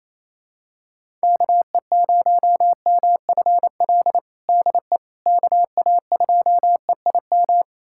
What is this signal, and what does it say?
Morse code, heard as a keyed tone, about 28 wpm, 700 Hz. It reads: KE0MFL DE KA2EIM
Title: USA Callsign DE Callsign - 6 Characters - 28wpm